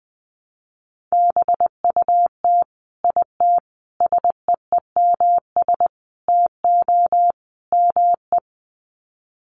Morse code, heard as a keyed tone, about 20 wpm, 700 Hz. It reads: BUT IT SEEMS TO ME
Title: Sets of 5 Words Encore2 20wpm